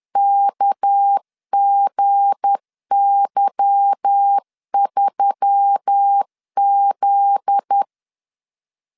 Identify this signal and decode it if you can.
Morse code, heard as a keyed tone, about 10 wpm, 800 Hz. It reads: KGY3Z
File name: LEZIONI / 8 YJ45 / 8 Y 10wpm 2615